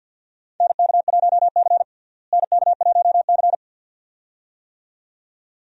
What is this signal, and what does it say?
Morse code, heard as a keyed tone, about 53 wpm, 700 Hz. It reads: NK1C NK1C
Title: USA Call Signs - 4 Characters - Repeated 2x - 50wpm